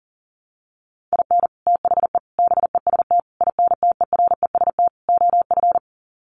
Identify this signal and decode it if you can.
Morse code, heard as a keyed tone, about 40 wpm, 700 Hz. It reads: IN THE BEST INTEREST OF
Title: Sets of 5 Words Encore 40wpm